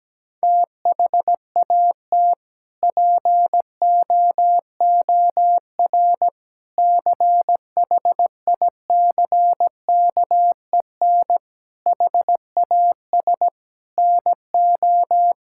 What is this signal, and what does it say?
Morse code, heard as a keyed tone, about 17 wpm, 700 Hz. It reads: THAT POOR CHICKEN HAS NO
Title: Sentences from Top 1000 Words - Review Part 5 - 17wpm